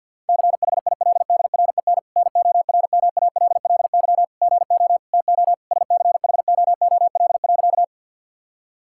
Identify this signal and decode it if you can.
Morse code, heard as a keyed tone, about 50 wpm, 700 Hz. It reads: CHILDREN NORMALLY GO TO SCHOOL.
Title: Sentences from Top 1000 Words - Review Part 6 - 50wpm